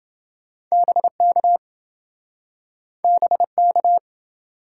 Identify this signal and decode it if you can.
Morse code, heard as a keyed tone, about 26 wpm, 700 Hz. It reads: BK BK